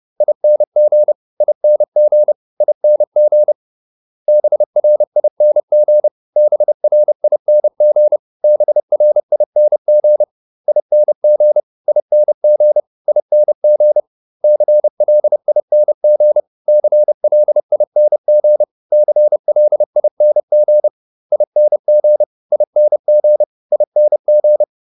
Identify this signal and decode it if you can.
Morse code, heard as a keyed tone, about 30 wpm, 600 Hz. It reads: ING ING ING BRING BRING BRING ING ING ING CLING CLING CLING ING ING ING